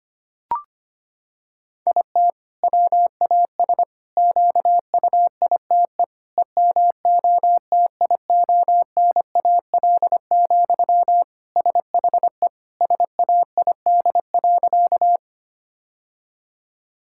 Morse code, heard as a keyed tone, about 25 wpm, 700 Hz. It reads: IT WAS QUITE EMOTIONAL, SHE SAID.
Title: Sentences from Top 1000 Words - Review Part 5 - 25wpm